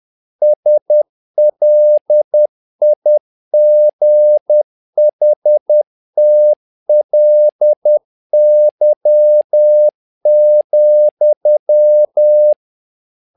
Morse code, heard as a keyed tone, about 10 wpm, 600 Hz. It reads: SLIGHTLY,
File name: learning center / tower trouble / tower trouble 10wpm